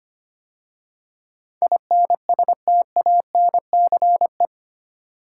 Morse code, heard as a keyed tone, about 25 wpm, 700 Hz. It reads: INSTANCE
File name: 8 Letter Words - First 3 Letters Given - 25wpm